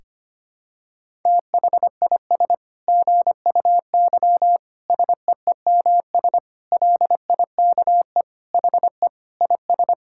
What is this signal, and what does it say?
Morse code, heard as a keyed tone, about 25 wpm, 700 Hz. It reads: THIS GUY SEEMS LIKE HE IS
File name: practice / 25wpm / Sentences with 9-Word Phrases - 25wpm